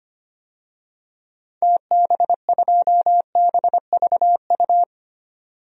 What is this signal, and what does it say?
Morse code, heard as a keyed tone, about 25 wpm, 700 Hz. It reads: TB2BVU
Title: DXCC-100 Call Signs - 6 Characters - 25wpm